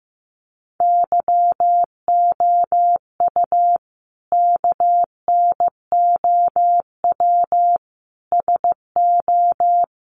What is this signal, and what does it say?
Morse code, heard as a keyed tone, about 15 wpm, 700 Hz. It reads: YOU KNOW SO
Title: Sentences from Top 800 Words 15wpm